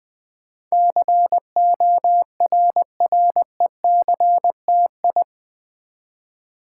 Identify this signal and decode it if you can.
Morse code, heard as a keyed tone, about 20 wpm, 700 Hz. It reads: CORRECTI
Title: Build A Word - 4 Letters At A Time - 10-Letter Words - 20wpm